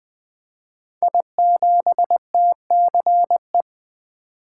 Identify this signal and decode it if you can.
Morse code, heard as a keyed tone, about 20 wpm, 700 Hz. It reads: I7TCE
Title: Code Groups - 5 Letter:Numbers - 20wpm